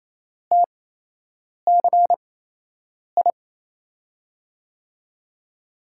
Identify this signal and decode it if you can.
Morse code, heard as a keyed tone, about 28 wpm, spaced as for 8 wpm, 700 Hz. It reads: TCI
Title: USA Call Signs - 6 Characters - 8x Character Spacing - 28wpm